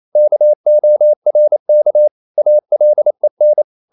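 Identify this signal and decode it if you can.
Morse code, heard as a keyed tone, about 28 wpm, 600 Hz. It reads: KORK ALEN
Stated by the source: Danish short words sent in Morse at 28 wpm, from CW med Gnister.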